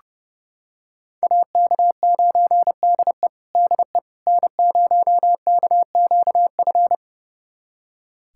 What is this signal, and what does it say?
Morse code, heard as a keyed tone, about 29 wpm, 700 Hz. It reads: AK9DE DE N0KQF
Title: USA Callsign DE Callsign - 5 Characters - 30wpm